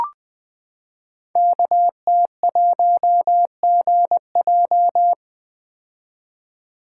Morse code, heard as a keyed tone, about 20 wpm, 700 Hz. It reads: KT1GJ
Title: Code Groups - 5 Letter:Numbers - 20wpm